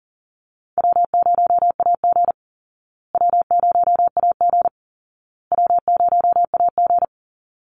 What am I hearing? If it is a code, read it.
Morse code, heard as a keyed tone, about 40 wpm, 700 Hz. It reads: W0AG W0AG W0AG
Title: Call Signs Repeated 2x - 40wpm